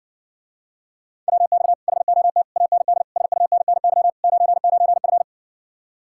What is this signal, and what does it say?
Morse code, heard as a keyed tone, about 60 wpm, 700 Hz. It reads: WX HOT AND SUNNY 99F